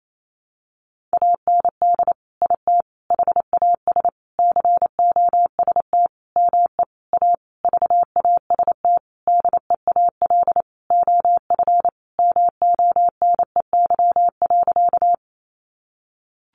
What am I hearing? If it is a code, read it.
Morse code, heard as a keyed tone, about 28 wpm, 700 Hz. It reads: AND IT HAS COST ME A VAST DEAL OF MONEY.
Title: Sentences from Top 1300 Words 28wpm